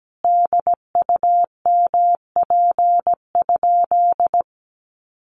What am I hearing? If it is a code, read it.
Morse code, heard as a keyed tone, about 17 wpm, 700 Hz. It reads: DUMP?
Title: Sentences from Top 1300 Words 17wpm